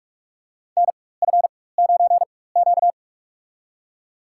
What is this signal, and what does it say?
Morse code, heard as a keyed tone, about 45 wpm, spaced as for 21 wpm, 700 Hz. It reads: NF9Q
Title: USA Call Signs - 4 Characters - 4x Character Spacing - 45wpm